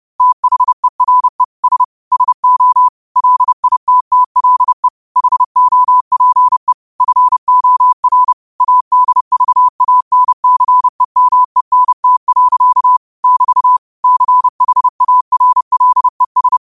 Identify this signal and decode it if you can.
Morse code, heard as a keyed tone, about 30 wpm, 1000 Hz. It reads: THERE S SO LITTLE HOPE FOR ADVANCEMENT. <BT> CHARLES